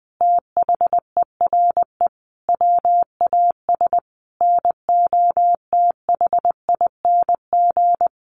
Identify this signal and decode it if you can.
Morse code, heard as a keyed tone, about 20 wpm, 700 Hz. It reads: THERE WAS NOTHING
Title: Sentences from Top 1000 Words 20wpm